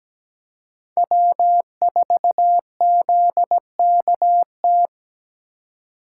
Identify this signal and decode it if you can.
Morse code, heard as a keyed tone, about 17 wpm, 700 Hz. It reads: W4ZKT
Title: USA Call Signs - 3x Character Spacing - 17wpm